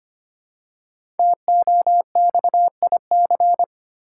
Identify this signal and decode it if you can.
Morse code, heard as a keyed tone, about 25 wpm, 700 Hz. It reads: TOXIC
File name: 5 Letter Words - First 3 Letters Given - 25wpm